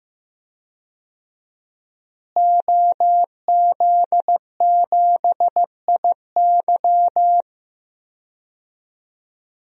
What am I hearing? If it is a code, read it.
Morse code, heard as a keyed tone, about 15 wpm, 700 Hz. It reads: OZ7IY
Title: DXCC-100 Call Signs - 5 Characters - Repeated 3x - 15wpm